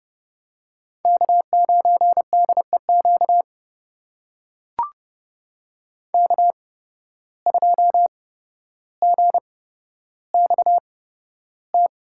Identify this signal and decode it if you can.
Morse code, heard as a keyed tone, about 30 wpm, spaced as for 13 wpm, 700 Hz. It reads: K9DEQ K2GXT
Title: USA Call Signs - 5 Characters - 8x Character Spacing - 30wpm